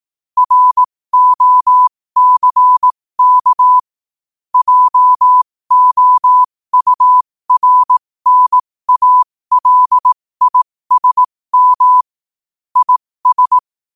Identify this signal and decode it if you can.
Morse code, heard as a keyed tone, about 18 wpm, spaced as for 15 wpm, 1000 Hz. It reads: ROCK JOURNALISM IS
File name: qotdfiles / 15wpm